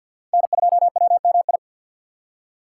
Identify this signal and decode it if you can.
Morse code, heard as a keyed tone, about 53 wpm, 700 Hz. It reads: N2WMI
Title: USA Call Signs - 5 Characters - 2x Character Spacing - 50wpm